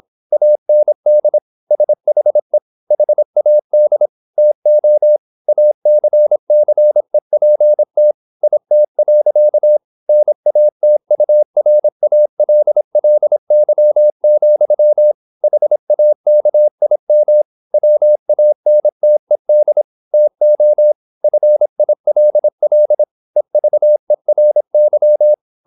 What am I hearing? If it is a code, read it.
Morse code, heard as a keyed tone, about 26 wpm, 600 Hz. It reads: AND SHE HAD TO ACCEPT IT. NATURALLY, HAKIM WANTED TO FILL EVERY
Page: Never 26wpm | CW med Gnister